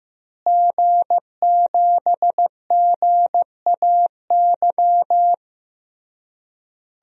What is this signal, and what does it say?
Morse code, heard as a keyed tone, about 16 wpm, 700 Hz. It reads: G7GAY
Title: DXCC-100 Call Signs - 5 Characters - Repeated 3x - 15wpm